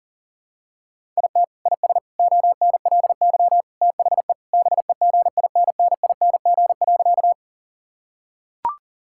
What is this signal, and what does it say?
Morse code, heard as a keyed tone, about 40 wpm, 700 Hz. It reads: IT IS ONLY THE BEGINNING.
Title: Sentences from Top 1000 Words - Review Part 5 - 40wpm